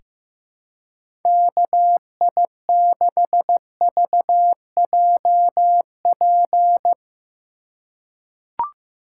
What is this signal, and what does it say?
Morse code, heard as a keyed tone, about 16 wpm, 700 Hz. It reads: KI6VJP
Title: USA Call Signs - 6 Characters - 6x Character Spacing - 15wpm